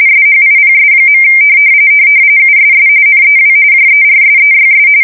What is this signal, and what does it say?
RTTY, about 45.45 baud. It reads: SHIPS. USS NEW JERSEYS
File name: fsk170 / USS-New-Jersey